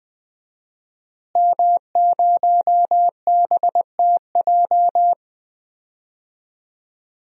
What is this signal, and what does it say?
Morse code, heard as a keyed tone, about 20 wpm, 700 Hz. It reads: M0BTJ
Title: DXCC-100 Call Signs - 5 Characters - Repeated 3x - 20wpm